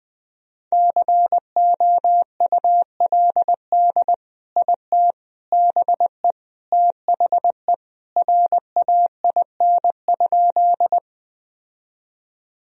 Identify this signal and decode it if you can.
Morse code, heard as a keyed tone, about 21 wpm, 700 Hz. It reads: COULD IT BE THE RAIN?
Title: Sentences from Top 1000 Words - Review Part 1 - 20wpm